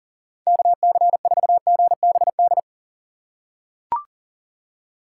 Morse code, heard as a keyed tone, about 40 wpm, 700 Hz. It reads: KC4GBD
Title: USA Call Signs - 6 Characters - 6x Character Spacing - 40wpm